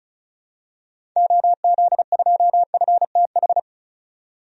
Morse code, heard as a keyed tone, about 34 wpm, 700 Hz. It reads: OZ2FTH
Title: DXCC-100 Call Signs - 6 Characters - 35wpm